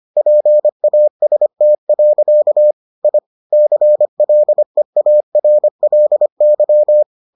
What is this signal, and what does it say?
Morse code, heard as a keyed tone, about 25 wpm, 600 Hz. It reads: PAST. I CLEARLY